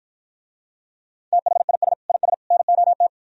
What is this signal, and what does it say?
Morse code, heard as a keyed tone, about 52 wpm, 700 Hz. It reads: THIS IS NOT